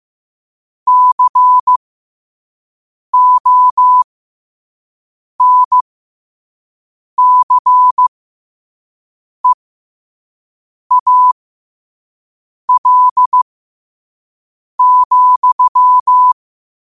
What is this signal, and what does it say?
Morse code, heard as a keyed tone, about 15 wpm, spaced as for 5 wpm, 1000 Hz. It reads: CONCEAL,